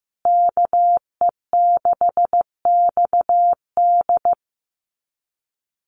Morse code, heard as a keyed tone, about 15 wpm, 700 Hz. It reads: KE6XD
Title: USA Call Signs Repeated 3x 15wpm